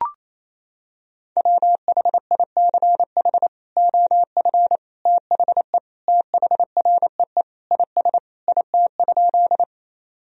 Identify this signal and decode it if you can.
Morse code, heard as a keyed tone, about 28 wpm, 700 Hz. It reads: WHICH OF THE THREE IS IT?
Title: Sentences from Top 1000 Words - Review Part 1 - 28wpm